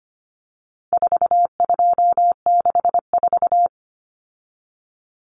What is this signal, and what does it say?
Morse code, heard as a keyed tone, about 25 wpm, 700 Hz. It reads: 4264
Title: Numbers - 4-Digit 25wpm